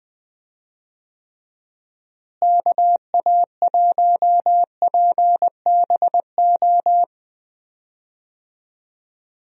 Morse code, heard as a keyed tone, about 20 wpm, 700 Hz. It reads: KA1PBO
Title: USA Call Signs - 6 Characters - Repeated 2x - 20wpm